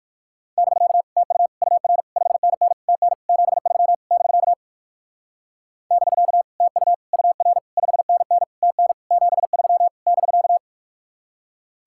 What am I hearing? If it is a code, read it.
Morse code, heard as a keyed tone, about 60 wpm, 700 Hz. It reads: <BK> TU UR 5NN TN 73 <BK> <BK> TU UR 5NN TN 73 <BK>